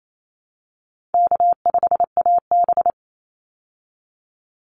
Morse code, heard as a keyed tone, about 28 wpm, 700 Hz. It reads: K5AB
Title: Easy Call Signs Repeated 3x - 28wpm